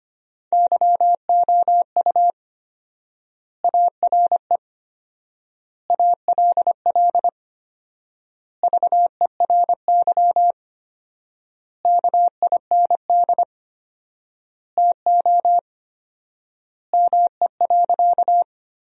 Morse code, heard as a keyed tone, about 25 wpm, 700 Hz. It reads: YOU ARE ALL VERY KIND TO ME.
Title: 4x Word-Spacing - Sentences from Top 200 Words - 25wpm